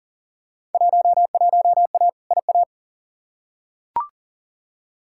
Morse code, heard as a keyed tone, about 39 wpm, 700 Hz. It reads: 11A IA
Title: Field Day Exchanges Encore - 40wpm